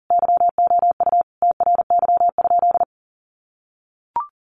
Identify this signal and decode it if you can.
Morse code, heard as a keyed tone, about 37 wpm, 700 Hz. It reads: YOU TRY?
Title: Sentences from Top 200 Words 40wpm